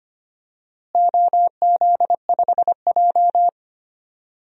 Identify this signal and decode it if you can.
Morse code, heard as a keyed tone, about 25 wpm, 700 Hz. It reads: OZ5J
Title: DXCC-100 Call Signs - 4 Characters - Repeated 2x - 25wpm